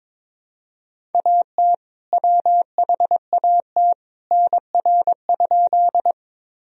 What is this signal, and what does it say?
Morse code, heard as a keyed tone, about 22 wpm, 700 Hz. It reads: AT WHAT NR?
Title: Sentences from Top 100 Words and Abbreviations 22wpm